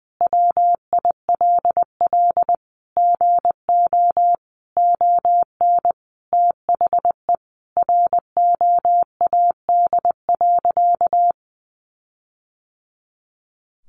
Morse code, heard as a keyed tone, about 20 wpm, 700 Hz. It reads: WILL GO ON THE ROAD.
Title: Sentences from Top 1400 Words 20wpm